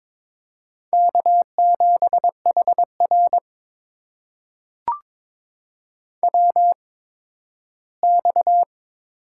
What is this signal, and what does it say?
Morse code, heard as a keyed tone, about 22 wpm, spaced as for 13 wpm, 700 Hz. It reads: K7HR WX
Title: USA Call Signs - 4 Characters - 8x Character Spacing - 22wpm